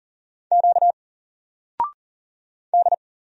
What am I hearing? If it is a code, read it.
Morse code, heard as a keyed tone, about 40 wpm, 700 Hz. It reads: Q D
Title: Single Letter-Number 40wpm